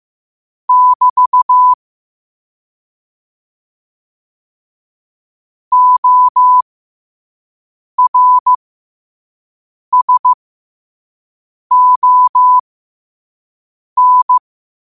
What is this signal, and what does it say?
Morse code, heard as a keyed tone, about 15 wpm, spaced as for 5 wpm, 1000 Hz. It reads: <BT> ORSON
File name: qotdfiles / 5wpm